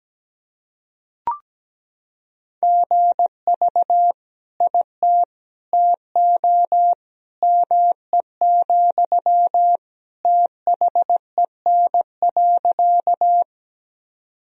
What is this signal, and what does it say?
Morse code, heard as a keyed tone, about 17 wpm, 700 Hz. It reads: GV IT TO ME, THEN.
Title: Sentences from Top 200 Words and Abbreviations 17wpm